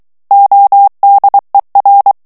Morse code, heard as a keyed tone, about 23 wpm, 800 Hz. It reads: ODER
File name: morse mp3 / ALLEMAND texte clair 1 vitesse 22 mots minute